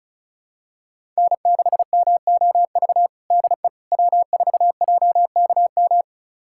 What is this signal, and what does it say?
Morse code, heard as a keyed tone, about 36 wpm, 690 Hz. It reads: N6MOV DE W4JKM
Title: USA Callsign DE Callsign - 5 Characters - 35wpm